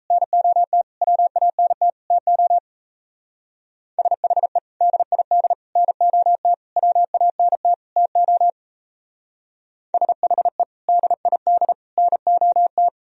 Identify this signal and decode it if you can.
Morse code, heard as a keyed tone, about 35 wpm, 700 Hz. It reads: NOT WANT TO SHE DID NOT WANT TO SHE DID NOT
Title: Speed-Racing - Sets of 5 Words 28wpm